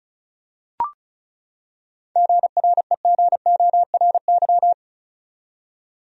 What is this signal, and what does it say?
Morse code, heard as a keyed tone, about 36 wpm, 700 Hz. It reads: GREGORY
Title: USA Common Names - 35wpm